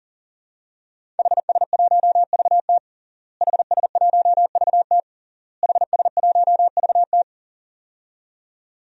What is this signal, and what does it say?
Morse code, heard as a keyed tone, about 40 wpm, 700 Hz. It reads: HS1VT HS1VT HS1VT
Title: DXCC-100 Call Signs - 5 Characters - Repeated 3x - 40wpm